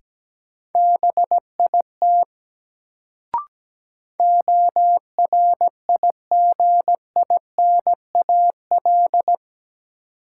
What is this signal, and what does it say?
Morse code, heard as a keyed tone, about 17 wpm, 700 Hz. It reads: BIT ORIGINAL
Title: Sentences from Top 900 Words - New Word Drill - 17wpm